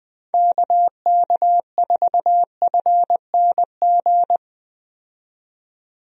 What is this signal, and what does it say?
Morse code, heard as a keyed tone, about 20 wpm, 700 Hz. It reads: KK4FNG
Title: USA Call Signs - 6 Characters - 20wpm